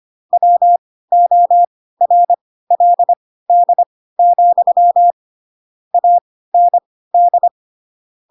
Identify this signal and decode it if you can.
Morse code, heard as a keyed tone, about 25 wpm, spaced as for 16 wpm, 700 Hz. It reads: WORLD, AND
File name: Pinocchio / Pinocchio ch4 10 25 ik1hge